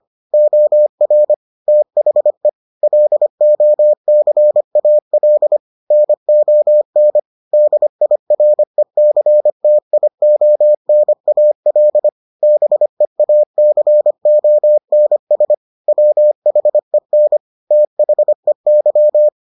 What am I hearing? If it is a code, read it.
Morse code, heard as a keyed tone, about 25 wpm, 600 Hz. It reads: OR THE LOCAL NON DIRECTIONAL BEACONS WHEN THEY